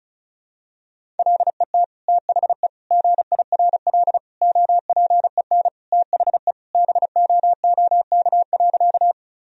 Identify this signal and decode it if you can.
Morse code, heard as a keyed tone, about 34 wpm, 700 Hz. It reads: LET THE GIRL OPEN THE BOOK.
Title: Sentences from Top 1000 Words - Review Part 1 - 35wpm